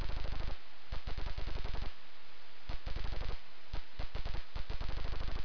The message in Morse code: none

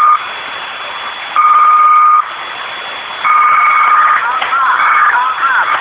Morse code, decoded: none

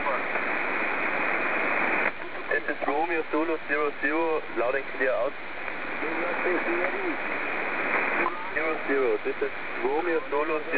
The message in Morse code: none

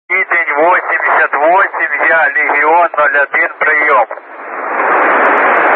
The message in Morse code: none

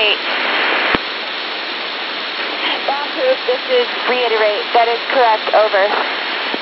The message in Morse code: none

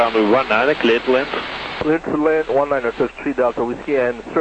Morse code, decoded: none

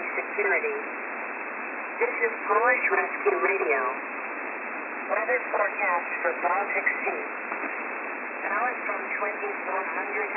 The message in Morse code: none

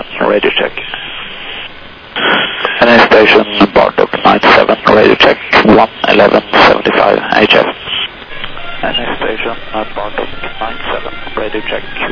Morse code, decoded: none

